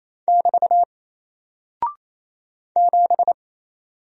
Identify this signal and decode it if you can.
Morse code, heard as a keyed tone, about 27 wpm, 700 Hz. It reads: <BT> 7
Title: Single Character (Letter:Number:Punctuation:Prosign) Alt 28wpm